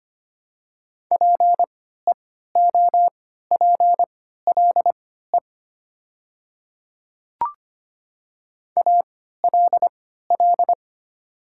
Morse code, heard as a keyed tone, about 25 wpm, spaced as for 14 wpm, 700 Hz. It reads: PEOPLE ALL
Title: 3x Character Spacing - Top 100 Words - 25wpm